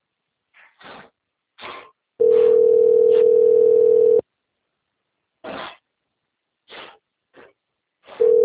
Ringback tone (North America), 440 + 480 Hz.